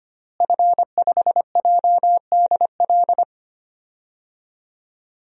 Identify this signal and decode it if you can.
Morse code, heard as a keyed tone, about 25 wpm, 700 Hz. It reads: F5JDL
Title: DXCC-100 Call Signs - 5 Characters - Repeated 2x - 25wpm